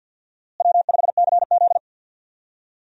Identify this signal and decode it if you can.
Morse code, heard as a keyed tone, about 52 wpm, 700 Hz. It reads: W5CZ
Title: USA Call Signs - 4 Characters - Repeated 2x - 50wpm